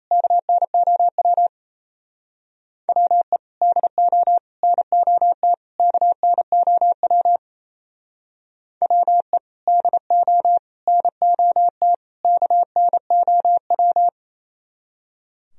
Morse code, heard as a keyed tone, about 33 wpm, 700 Hz. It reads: KNOW WE DO NOT KNOW WE DO NOT KNOW
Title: Speed-Racing - Sets of 4 Words 28wpm